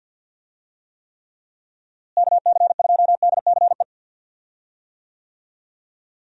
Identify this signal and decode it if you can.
Morse code, heard as a keyed tone, about 53 wpm, 700 Hz. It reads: KC2DCE
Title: USA Call Signs - 6 Characters - Repeated 3x - 50wpm